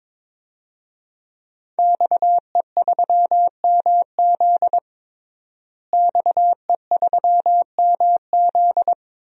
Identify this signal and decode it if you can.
Morse code, heard as a keyed tone, about 22 wpm, 700 Hz. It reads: XE3MZ XE3MZ
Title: DXCC-100 Call Signs - 5 Characters - Repeated 2x - 22wpm